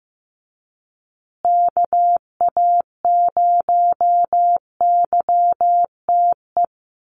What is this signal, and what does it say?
Morse code, heard as a keyed tone, about 15 wpm, 700 Hz. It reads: KA0YTE
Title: USA Call Signs Repeated 2x 15wpm